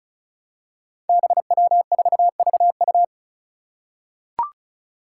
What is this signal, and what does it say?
Morse code, heard as a keyed tone, about 36 wpm, 700 Hz. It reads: BW4VU
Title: Five Challenging Letter-Number Combinations 35wpm